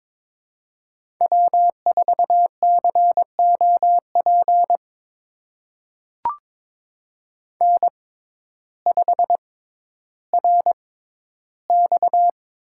Morse code, heard as a keyed tone, about 22 wpm, spaced as for 12 wpm, 700 Hz. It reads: W4COP N5RX
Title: USA Call Signs - 5 Characters - 6x Character Spacing - 22wpm